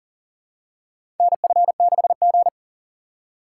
Morse code, heard as a keyed tone, about 40 wpm, 700 Hz. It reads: NF6G
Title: USA Call Signs - 3x Character Spacing - 40wpm